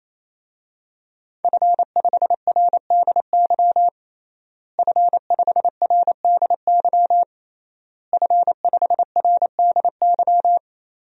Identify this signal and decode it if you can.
Morse code, heard as a keyed tone, about 28 wpm, 700 Hz. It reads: F5RDY F5RDY F5RDY
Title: DXCC-100 Call Signs - 5 Characters - Repeated 3x - 28wpm